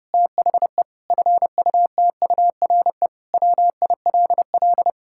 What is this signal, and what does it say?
Morse code, heard as a keyed tone, about 32 wpm, 700 Hz. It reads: THE FUTURE WILL
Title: Sentences from Top 1000 Words - Review Part 5 - 30wpm